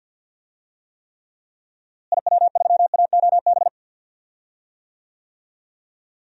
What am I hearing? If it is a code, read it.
Morse code, heard as a keyed tone, about 47 wpm, 700 Hz. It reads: IW3AOB
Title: DXCC-100 Call Signs - 6 Characters - Repeated 2x - 50wpm